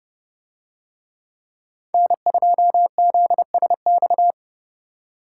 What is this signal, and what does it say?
Morse code, heard as a keyed tone, about 28 wpm, 700 Hz. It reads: N2ZSX